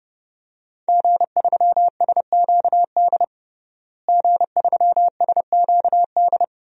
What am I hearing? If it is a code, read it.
Morse code, heard as a keyed tone, about 30 wpm, 700 Hz. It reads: G3SQD G3SQD
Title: DXCC-100 Call Signs - 5 Characters - Repeated 3x - 30wpm